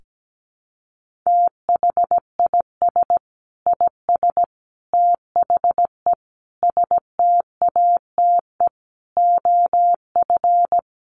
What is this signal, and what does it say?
Morse code, heard as a keyed tone, about 17 wpm, 700 Hz. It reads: THIS IS THE STATE OF
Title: Sentences from Top 1400 Words 17wpm